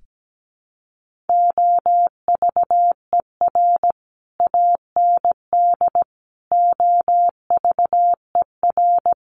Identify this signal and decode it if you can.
Morse code, heard as a keyed tone, about 17 wpm, 700 Hz. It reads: OVER AND OVER
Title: Sets of 4 Words 17wpm